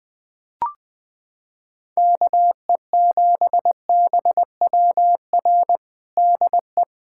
Morse code, heard as a keyed tone, about 20 wpm, 700 Hz. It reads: KE7BWR DE
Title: USA Callsign DE Callsign - 6 Characters - 20wpm